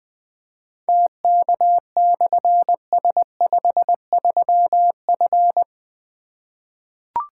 Morse code, heard as a keyed tone, about 21 wpm, 700 Hz. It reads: TK/S53F
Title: Difficult Call Signs 20wpm